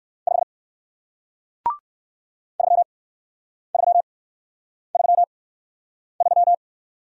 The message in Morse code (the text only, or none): 5 3333